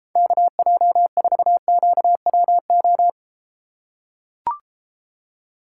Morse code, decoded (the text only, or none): KJ4QWO